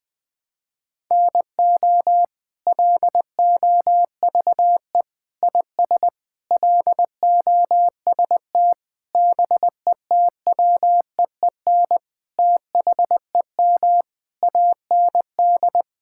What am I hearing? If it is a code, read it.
Morse code, heard as a keyed tone, about 20 wpm, 700 Hz. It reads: NO LOVE IS LOST BETWEEN THEM AND
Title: Sentences from Top 1600 Words 20wpm